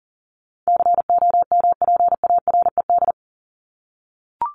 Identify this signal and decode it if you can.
Morse code, heard as a keyed tone, about 40 wpm, 700 Hz. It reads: COMPARED
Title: ED Suffix - Top 100 Words with Base Words 40wpm